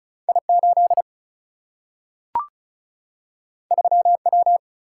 Morse code, heard as a keyed tone, about 36 wpm, 700 Hz. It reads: I 8 3 W
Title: Single-Letter Single-Number 35wpm